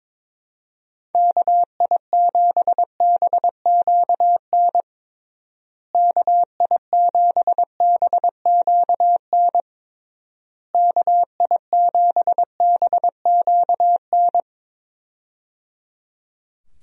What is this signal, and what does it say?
Morse code, heard as a keyed tone, about 22 wpm, 700 Hz. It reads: KI7BQN KI7BQN KI7BQN
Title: USA Call Signs - 6 Characters - Repeated 3x - 22wpm